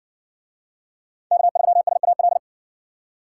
Morse code, heard as a keyed tone, about 55 wpm, 700 Hz. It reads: K3SAL